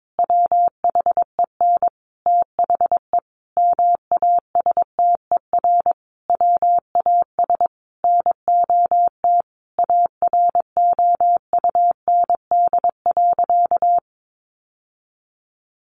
Morse code, heard as a keyed tone, about 22 wpm, 700 Hz. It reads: WHEN THE MASTER WAS NOT AROUND.
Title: Sentences from Top 1300 Words 22wpm